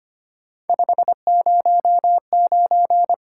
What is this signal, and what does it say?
Morse code, heard as a keyed tone, about 25 wpm, 700 Hz. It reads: 509
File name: Three Letter-Number Combinations - 25wpm